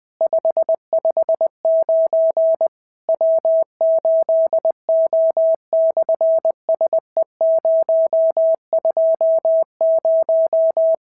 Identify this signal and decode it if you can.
Morse code, heard as a keyed tone, about 20 wpm, 650 Hz. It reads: 559 W8O/SE020
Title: SOTA QSO Practice 20wpm